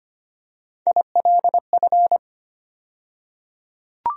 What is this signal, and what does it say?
Morse code, heard as a keyed tone, about 25 wpm, 700 Hz. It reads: ILF
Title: USA Call Sign Postfixes - 3 Letters 25wpm